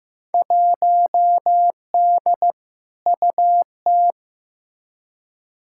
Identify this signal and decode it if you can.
Morse code, heard as a keyed tone, about 15 wpm, 700 Hz. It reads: 1D UT
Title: Field Day Exchanges - 15wpm